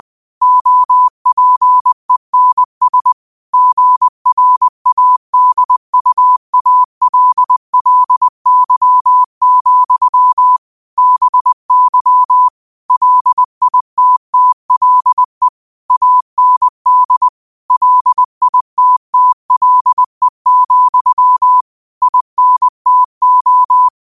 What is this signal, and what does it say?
Morse code, heard as a keyed tone, about 20 wpm, 1000 Hz. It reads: OPENS GRADUALLY, BY LITTLE AND LITTLE, INTO
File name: qotdfiles / 20wpm